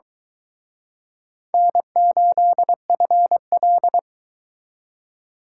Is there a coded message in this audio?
Yes: Morse code from 1.5 to 4.0 s, heard as a keyed tone, about 23 wpm, 700 Hz. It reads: N8FL